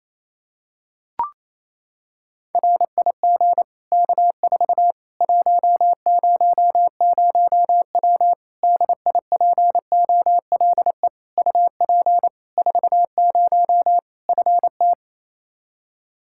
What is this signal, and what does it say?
Morse code, heard as a keyed tone, about 28 wpm, 700 Hz. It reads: RIG K4 100W DIPOLE UP 40 FT
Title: Practice QSO - Exchange 2 - Info - 28wpm